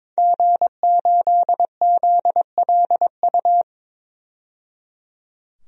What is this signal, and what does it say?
Morse code, heard as a keyed tone, about 22 wpm, 700 Hz. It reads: G8ZLU
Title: DXCC-100 Call Signs - 5 Characters - 22wpm